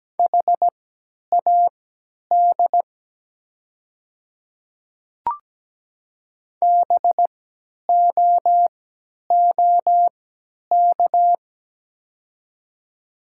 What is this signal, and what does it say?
Morse code, heard as a keyed tone, about 17 wpm, spaced as for 10 wpm, 700 Hz. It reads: HAD BOOK
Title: 3x Character Spacing - Top 300 Words - 17wpm